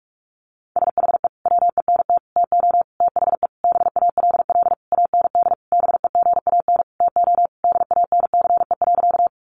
Morse code, heard as a keyed tone, about 44 wpm, 700 Hz. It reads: SHE WENT TO THE BALL AND BEGAN TO DANCE.
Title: Sentences from Top 900 Words 45wpm